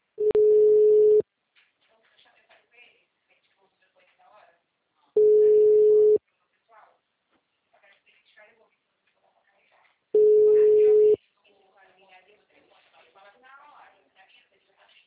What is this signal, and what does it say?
Ringback tone (Europe, 425 Hz), with speech.
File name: out- -2013- - - .51238